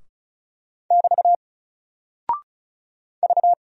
Morse code, heard as a keyed tone, about 34 wpm, 700 Hz. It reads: <BT> V
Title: Single Character (Letter:Number:Punctuation:Prosign) Alt 35wpm